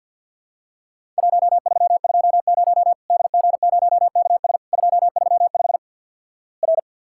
Morse code, heard as a keyed tone, about 54 wpm, 700 Hz. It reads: 1320 DG0KS 235 R
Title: Worked All Europe DX Contest - QTC Exchange - 50wpm